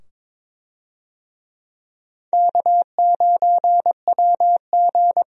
Morse code, heard as a keyed tone, about 22 wpm, 700 Hz. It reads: K9WG
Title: USA Call Signs - 4 Characters - 22wpm